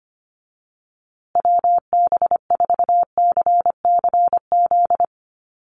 Morse code, heard as a keyed tone, about 25 wpm, 700 Hz. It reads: WB4CCZ
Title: USA Call Signs Repeated 2x 25wpm